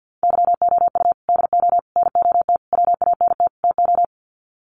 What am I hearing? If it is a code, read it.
Morse code, heard as a keyed tone, about 48 wpm, 700 Hz. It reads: YOU DO NOT WANT TO
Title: Sets of 5 Words 50wpm